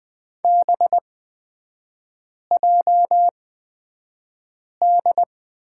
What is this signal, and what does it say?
Morse code, heard as a keyed tone, about 20 wpm, spaced as for 5 wpm, 700 Hz. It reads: BJD
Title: 224 Lesson 36e - BT - Three Characters 20wpm